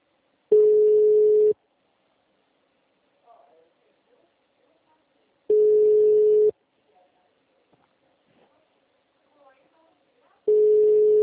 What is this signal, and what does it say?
Ringback tone (Europe, 425 Hz), with speech.